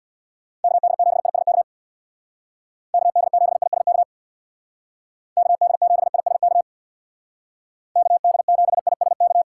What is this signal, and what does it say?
Morse code, heard as a keyed tone, about 62 wpm, 700 Hz. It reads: KD7ISK KD7ISK KD7ISK KD7ISK
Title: Speed Racing - USA Call Signs 50wpm